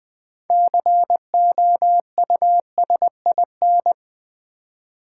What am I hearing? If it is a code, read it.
Morse code, heard as a keyed tone, about 20 wpm, 700 Hz. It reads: COUSIN
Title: Six Letter Words - 20wpm